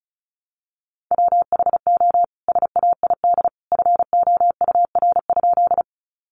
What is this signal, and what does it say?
Morse code, heard as a keyed tone, about 34 wpm, 700 Hz. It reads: WHO SAID FOUR?
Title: Sentences from Top 300 Words 35wpm